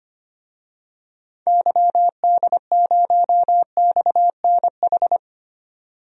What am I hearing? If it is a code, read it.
Morse code, heard as a keyed tone, about 25 wpm, 700 Hz. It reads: YD0XNH
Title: DXCC-100 Call Signs - 6 Characters - 25wpm